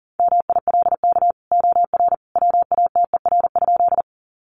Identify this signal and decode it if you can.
Morse code, heard as a keyed tone, about 40 wpm, 700 Hz. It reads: MILK OR WATER?
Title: Sentences from Top 1500 Words 40wpm